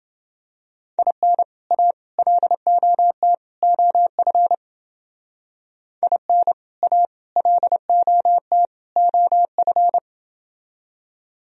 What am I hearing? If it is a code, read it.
Morse code, heard as a keyed tone, about 30 wpm, 700 Hz. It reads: IN A LOT OF IN A LOT OF